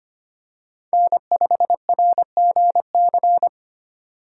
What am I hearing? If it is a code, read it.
Morse code, heard as a keyed tone, about 25 wpm, 700 Hz. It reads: N5RGC
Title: USA Call Signs - 5 Characters - 3x Character Spacing - 25wpm